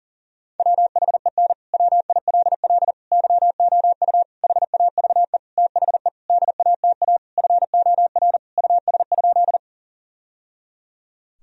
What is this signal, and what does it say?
Morse code, heard as a keyed tone, about 40 wpm, 700 Hz. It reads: WHEN WILL YOU HAVE THE DATA FOR US?
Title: Sentences from Top 1000 Words - Review Part 3 - 40wpm